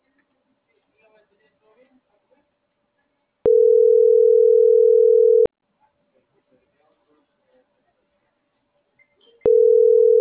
Ringback tone (North America), 440 + 480 Hz.